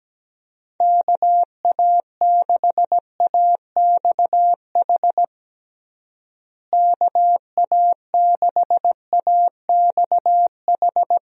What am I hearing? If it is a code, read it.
Morse code, heard as a keyed tone, about 17 wpm, 700 Hz. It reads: KA6AXH KA6AXH
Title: USA Call Signs - 6 Characters - Repeated 3x - 17wpm